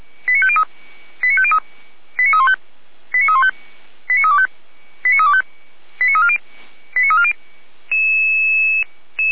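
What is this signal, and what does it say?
Five-tone selective call (ZVEI), heard as quick runs of five tones, tones of about 70 ms. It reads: 89503 89503 89315 89315 89325 89325 89340 89340
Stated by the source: ZVEI five-tone selective call 89230, received by SDR, from Feuerwehr Orlamünde (fire brigade) paging archive.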